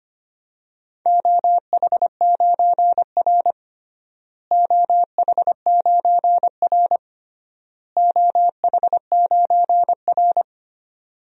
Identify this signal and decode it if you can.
Morse code, heard as a keyed tone, about 25 wpm, 700 Hz. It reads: OH9R OH9R OH9R
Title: DXCC-100 Call Signs - 4 Characters - Repeated 3x - 25wpm